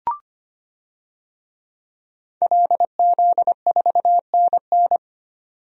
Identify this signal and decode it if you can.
Morse code, heard as a keyed tone, about 25 wpm, 690 Hz. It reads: LZ4NN
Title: DXCC-100 Call Signs - 5 Characters - 25wpm